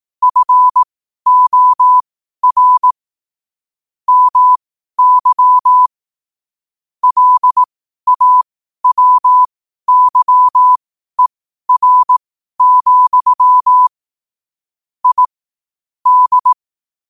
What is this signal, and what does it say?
Morse code, heard as a keyed tone, about 18 wpm, spaced as for 12 wpm, 1000 Hz. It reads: FOR MY LAWYER, I D
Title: 12 WPM morse code quotes for Thu, 29 Feb 2024